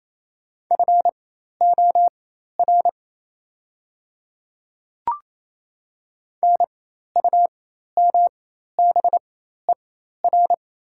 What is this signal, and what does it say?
Morse code, heard as a keyed tone, about 28 wpm, spaced as for 13 wpm, 700 Hz. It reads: FOR NUMBER
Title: 4x Character Spacing - Top 200 Words - 28wpm